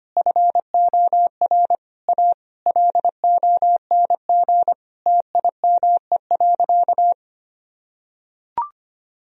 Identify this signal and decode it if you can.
Morse code, heard as a keyed tone, about 25 wpm, 700 Hz. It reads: FOR A LONG TIME.
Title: Sentences from Top 1000 Words - Review Part 1 - 25wpm